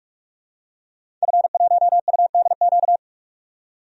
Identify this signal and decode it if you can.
Morse code, heard as a keyed tone, about 46 wpm, 700 Hz. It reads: F1UDQ
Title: DXCC-100 Call Signs - 5 Characters - 45wpm